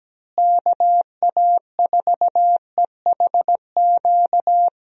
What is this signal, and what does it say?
Morse code, heard as a keyed tone, about 17 wpm, 700 Hz. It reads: KA4EHQ
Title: USA Call Signs - 6 Characters - Repeated 3x - 17wpm